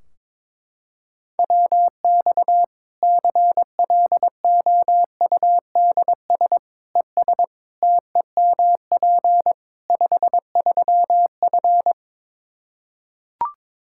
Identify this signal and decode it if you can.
Morse code, heard as a keyed tone, about 22 wpm, 700 Hz. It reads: WX CLOUDS ES TEMP 53F
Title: Practice QSO - Exchange 2 - Info - 22wpm